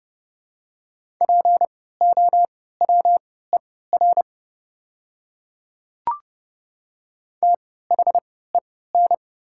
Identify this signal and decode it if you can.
Morse code, heard as a keyed tone, about 30 wpm, spaced as for 17 wpm, 700 Hz. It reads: POWER THEN
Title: 3x Character Spacing - Top 500 Words - 30wpm